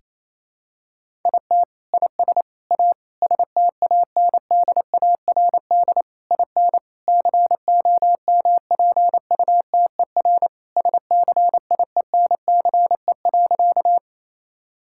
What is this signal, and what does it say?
Morse code, heard as a keyed tone, about 28 wpm, 700 Hz. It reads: IT IS A STANDARD IN COMPUTER SCIENCE.
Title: Sentences from Top 1000 Words - Review Part 6 - 28wpm